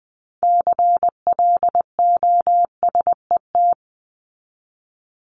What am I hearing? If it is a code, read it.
Morse code, heard as a keyed tone, about 20 wpm, 700 Hz. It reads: CLOSET
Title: Top 1751-2000 Words 20wpm